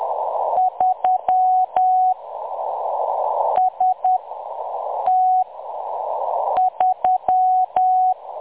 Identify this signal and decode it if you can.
Morse code, heard as a keyed tone, about 10 wpm, spaced as for 6 wpm, 740 Hz. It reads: 3 ST3
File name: M23-ST3- -0858utc-10184khz